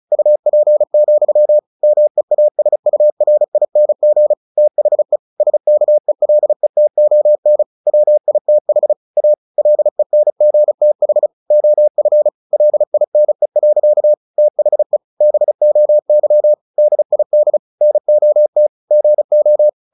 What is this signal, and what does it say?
Morse code, heard as a keyed tone, about 34 wpm, 600 Hz. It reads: UP, MEASURING THE SKELETON WITH A LENGTH OF LINE. THE BOY DID NOT GO
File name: the old man and the sea 2 35wpm